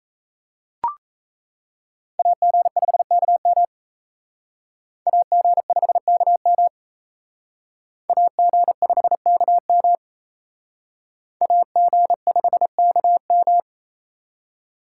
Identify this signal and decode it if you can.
Morse code, heard as a keyed tone, about 35 wpm, 700 Hz. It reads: AG5KM AG5KM AG5KM AG5KM